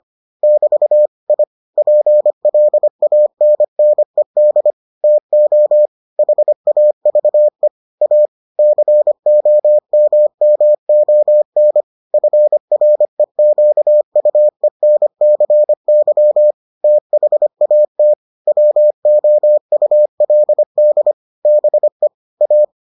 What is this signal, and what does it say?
Morse code, heard as a keyed tone, about 25 wpm, 600 Hz. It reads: <BT> I PLANNED TO HAVE A COMMON FREQUENCY THAT WOULD BE A